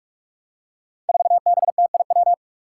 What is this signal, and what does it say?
Morse code, heard as a keyed tone, about 43 wpm, 700 Hz. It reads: 4BTIW
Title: Five Challenging Letter-Number Combinations 45wpm